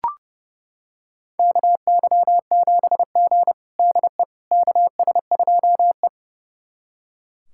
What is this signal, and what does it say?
Morse code, heard as a keyed tone, about 29 wpm, 700 Hz. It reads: KY7G DE KS2E
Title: USA Callsign DE Callsign - 4 Characters - 30wpm